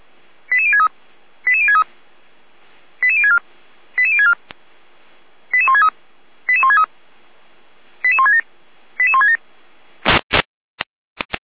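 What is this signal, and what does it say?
Five-tone selective call (ZVEI), heard as quick runs of five tones, tones of about 70 ms. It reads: 88063 88063 88064 88064 88163 88163 88167 88167
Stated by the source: ZVEI five-tone selective call received via SDR.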